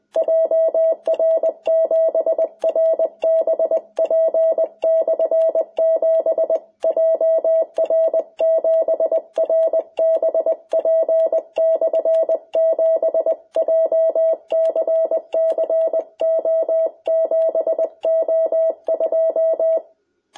Morse code, heard as a keyed tone, about 20 wpm, 660 Hz. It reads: JR7RBP/7 JR7RBP/7 JCC O7O2
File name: jh8cbh / 2021cw